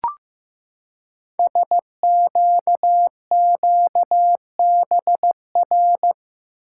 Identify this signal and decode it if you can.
Morse code, heard as a keyed tone, about 15 wpm, 700 Hz. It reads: SQQBR
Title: Code Groups - 5 Letters - 15wpm